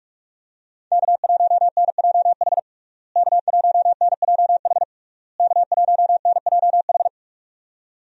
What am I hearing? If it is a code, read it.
Morse code, heard as a keyed tone, about 47 wpm, 700 Hz. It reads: K1NJH K1NJH K1NJH
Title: USA Call Signs - 5 Characters - Repeated 3x - 45wpm